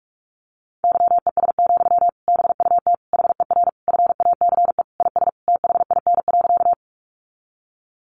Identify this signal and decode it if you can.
Morse code, heard as a keyed tone, about 44 wpm, 700 Hz. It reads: YES, BUT HER FACE IS THIN.
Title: Sentences from Top 1200 Words 45wpm